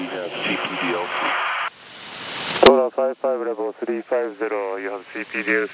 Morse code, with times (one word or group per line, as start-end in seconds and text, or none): none